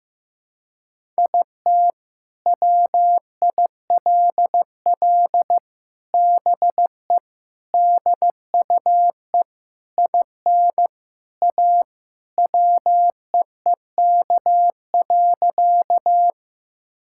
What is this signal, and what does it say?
Morse code, heard as a keyed tone, about 16 wpm, 700 Hz. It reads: IT WILL BE DUE IN A WEEK.
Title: Sentences from Top 1000 Words - Review Part 6 - 15wpm